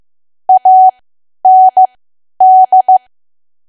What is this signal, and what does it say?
Morse code, heard as a keyed tone, about 15 wpm, spaced as for 10 wpm, 750 Hz. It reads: AND
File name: Archive / 10 WPM / 10WPM